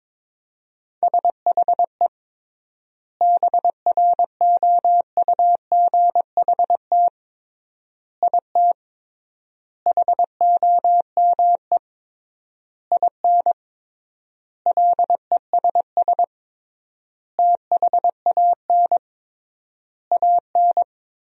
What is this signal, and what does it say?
Morse code, heard as a keyed tone, about 22 wpm, 700 Hz. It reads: SHE BROUGHT IT HOME IN LESS THAN AN
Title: 3x Word-Spacing - Sentences from Top 500 Words - 22wpm